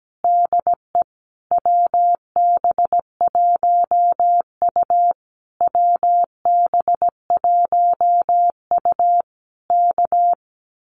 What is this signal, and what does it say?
Morse code, heard as a keyed tone, about 17 wpm, 700 Hz. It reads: DE WB1U WB1U K
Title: Common QSO Phrases Encore 17wpm